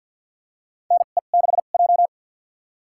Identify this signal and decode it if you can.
Morse code, heard as a keyed tone, about 47 wpm, spaced as for 35 wpm, 700 Hz. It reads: NE6J
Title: USA Call Signs - 4 Characters - 2x Character Spacing - 50wpm